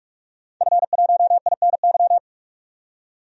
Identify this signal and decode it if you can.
Morse code, heard as a keyed tone, about 42 wpm, 700 Hz. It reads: F1INY
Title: DXCC-100 Call Signs - 5 Characters - 45wpm